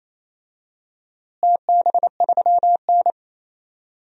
Morse code, heard as a keyed tone, about 28 wpm, 700 Hz. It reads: TB3N
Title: DXCC-100 Call Signs - 4 Characters - Repeated 2x - 28wpm